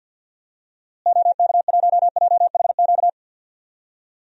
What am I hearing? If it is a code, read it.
Morse code, heard as a keyed tone, about 51 wpm, 700 Hz. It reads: OK1JHQ
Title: DXCC-100 Call Signs - 6 Characters - Repeated 2x - 50wpm